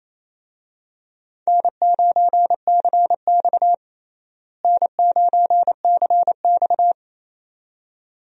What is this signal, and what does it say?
Morse code, heard as a keyed tone, about 28 wpm, 700 Hz. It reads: N9CX N9CX
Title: USA Call Signs - 4 Characters - Repeated 2x - 28wpm